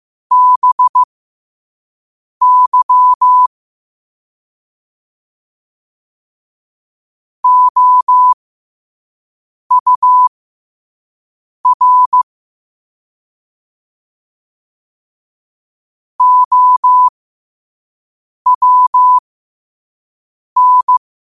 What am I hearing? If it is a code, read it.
Morse code, heard as a keyed tone, about 16 wpm, spaced as for 5 wpm, 1000 Hz. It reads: BY OUR OWN
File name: qotdfiles / 5wpm